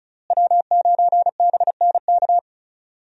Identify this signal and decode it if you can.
Morse code, heard as a keyed tone, about 34 wpm, 700 Hz. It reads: W9BNK
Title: USA Call Signs - 5 Characters - Repeated 2x - 35wpm